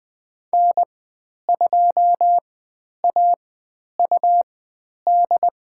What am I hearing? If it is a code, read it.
Morse code, heard as a keyed tone, about 20 wpm, spaced as for 10 wpm, 700 Hz. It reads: N2AUD
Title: 102 Lesson 17g - Letter u - Call Signs 20wpm